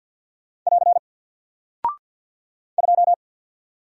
Morse code, heard as a keyed tone, about 50 wpm, 700 Hz. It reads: <AR> 2
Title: Single Character (Letter:Number:Punctuation:Prosign) Alt 50wpm